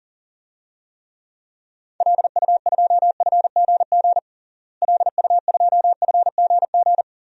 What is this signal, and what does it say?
Morse code, heard as a keyed tone, about 40 wpm, 700 Hz. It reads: LU2FGG LU2FGG
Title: DXCC-100 Call Signs - 6 Characters - Repeated 2x - 40wpm